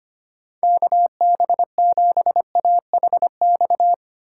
Morse code, heard as a keyed tone, about 25 wpm, 700 Hz. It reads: KB7AHX
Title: USA Call Signs - 6 Characters - 4x Character Spacing - 25wpm